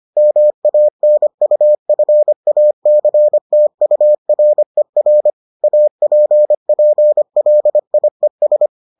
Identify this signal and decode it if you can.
Morse code, heard as a keyed tone, about 25 wpm, 600 Hz. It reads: MANUFACTURER APPLIES